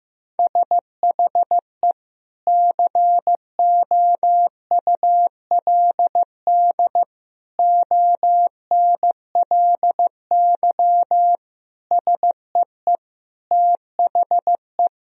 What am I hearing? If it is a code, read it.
Morse code, heard as a keyed tone, about 15 wpm, 700 Hz. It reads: SHE COULD ONLY SEE THE
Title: Sentences from Top 1000 Words - Review Part 2 - 15wpm